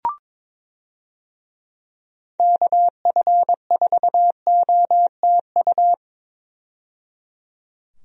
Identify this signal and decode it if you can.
Morse code, heard as a keyed tone, about 22 wpm, 700 Hz. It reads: KF4OTU
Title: USA Call Signs - 6 Characters - 22wpm